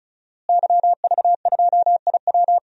Morse code, heard as a keyed tone, about 34 wpm, 700 Hz. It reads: YV2IW
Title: DXCC-100 Call Signs - 5 Characters - Repeated 2x - 35wpm